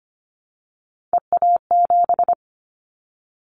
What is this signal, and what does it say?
Morse code, heard as a keyed tone, about 25 wpm, 700 Hz. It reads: EA7
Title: Callsign Prefixes for Top 25 DXCC - 25wpm